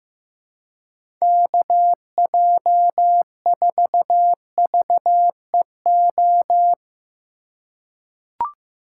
Morse code, heard as a keyed tone, about 15 wpm, 700 Hz. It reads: KJ4VEO
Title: USA Call Signs - 6 Characters - 4x Character Spacing - 15wpm